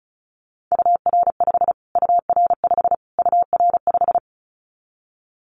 Morse code, heard as a keyed tone, about 34 wpm, 700 Hz. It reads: UR5 UR5 UR5
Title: Callsign Prefixes for Top 25 DXCC - Repeated 3x - 35wpm